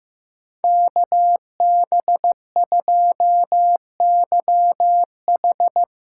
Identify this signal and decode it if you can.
Morse code, heard as a keyed tone, about 17 wpm, 700 Hz. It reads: KB2YH
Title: Speed Racing - USA 2x2 Call Signs 15wpm